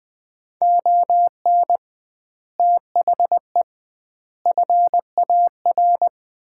Morse code, heard as a keyed tone, about 21 wpm, 700 Hz. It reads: ON THE FAR
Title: 2x Word-Spacing - Sets of 5 Words - 20wpm